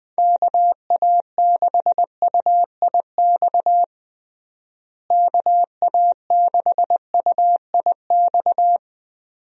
Morse code, heard as a keyed tone, about 21 wpm, 700 Hz. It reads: KA6UIX KA6UIX
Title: USA Call Signs - 6 Characters - Repeated 2x - 20wpm